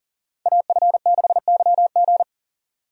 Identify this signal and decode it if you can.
Morse code, heard as a keyed tone, about 40 wpm, 700 Hz. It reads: AF6YG
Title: USA Call Signs - 5 Characters - 40wpm